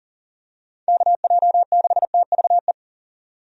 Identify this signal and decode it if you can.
Morse code, heard as a keyed tone, about 38 wpm, 700 Hz. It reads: KJ6TVE
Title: USA Call Signs - 6 Characters - 40wpm